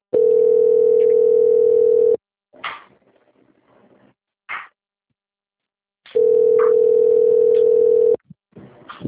Ringback tone (North America), 440 + 480 Hz.
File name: out- - - - - .57298